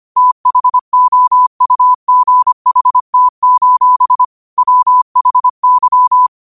Morse code, heard as a keyed tone, about 25 wpm, 1000 Hz. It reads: THOUGHT: WHY
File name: qotdfiles / 25wpm